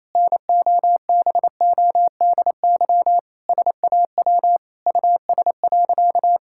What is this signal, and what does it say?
Morse code, heard as a keyed tone, about 28 wpm, 700 Hz. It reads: NOBODY SAW US.
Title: Sentences from Top 1600 Words 28wpm